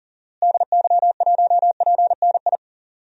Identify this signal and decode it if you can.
Morse code, heard as a keyed tone, about 40 wpm, 700 Hz. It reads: DY1PNI
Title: DXCC-100 Call Signs - 6 Characters - Repeated 2x - 40wpm